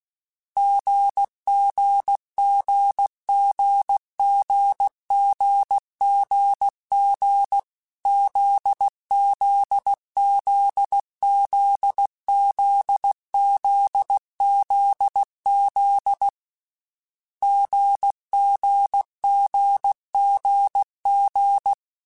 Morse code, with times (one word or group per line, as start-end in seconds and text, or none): 0.6-7.6 GGGGGGGG
8.1-16.3 ZZZZZZZZ
17.4-21.7 GGGGG